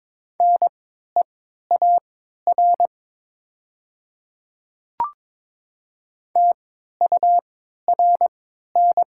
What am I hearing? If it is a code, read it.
Morse code, heard as a keyed tone, about 22 wpm, spaced as for 13 wpm, 700 Hz. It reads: NEAR TURN
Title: 3x Character Spacing - Top 200 Words - 22wpm